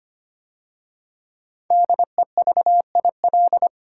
Morse code, heard as a keyed tone, about 25 wpm, 700 Hz. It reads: DEVIL
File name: 5 Letter Words - First 3 Letters Given - 25wpm